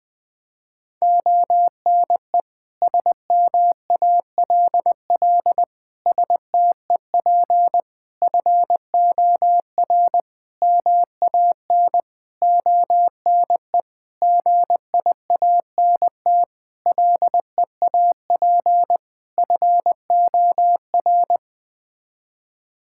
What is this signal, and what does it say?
Morse code, heard as a keyed tone, about 20 wpm, 700 Hz. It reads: ONE SMALL STEP FOR MAN ONE GIANT LEAP FOR
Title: Sets of 9 Words - 20wpm